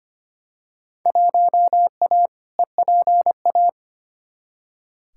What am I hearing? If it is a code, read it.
Morse code, heard as a keyed tone, about 25 wpm, 700 Hz. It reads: 1A EPA
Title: Field Day Exchanges - 25wpm